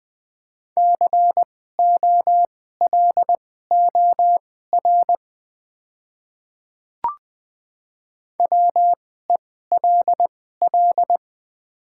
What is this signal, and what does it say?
Morse code, heard as a keyed tone, about 20 wpm, spaced as for 15 wpm, 700 Hz. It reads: COLOR WELL
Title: 2x Character Spacing - Top 500 Words - 20wpm